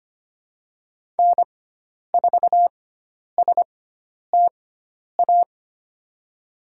Morse code, heard as a keyed tone, about 25 wpm, spaced as for 10 wpm, 700 Hz. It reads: N4STA
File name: Lesson 16g - Letter c - Call Signs 25wpm